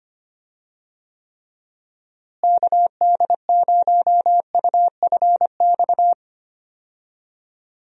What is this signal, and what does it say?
Morse code, heard as a keyed tone, about 25 wpm, 700 Hz. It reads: KD0UFX
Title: USA Call Signs - 6 Characters - Repeated 2x - 25wpm